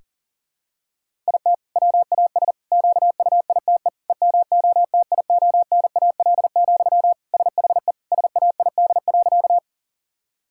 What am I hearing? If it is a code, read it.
Morse code, heard as a keyed tone, about 40 wpm, 700 Hz. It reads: IT WAS QUITE EMOTIONAL, SHE SAID.
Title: Sentences from Top 1000 Words - Review Part 5 - 40wpm